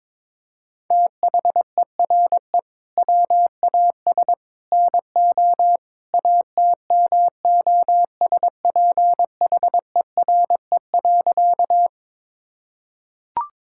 Morse code, heard as a keyed tone, about 22 wpm, 700 Hz. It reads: THERE WAS NO ATMOSPHERE.
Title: Sentences from Top 1600 Words 22wpm